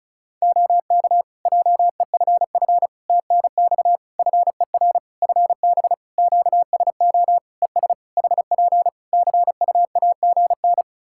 Morse code, heard as a keyed tone, about 36 wpm, 700 Hz. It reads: OK JEFF TNX FER FB QSO ES HP CUAGN
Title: Practice QSO - Exchange 4 - Ending - 35wpm